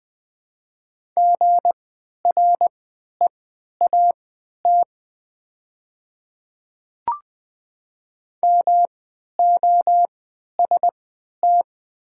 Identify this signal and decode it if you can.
Morse code, heard as a keyed tone, about 20 wpm, spaced as for 11 wpm, 700 Hz. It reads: GREAT MOST
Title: 3x Character Spacing - Top 200 Words - 20wpm